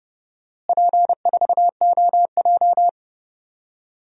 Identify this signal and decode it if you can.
Morse code, heard as a keyed tone, about 32 wpm, 700 Hz. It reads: P4OJ
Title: DXCC-100 Call Signs - 4 Characters - Repeated 2x - 30wpm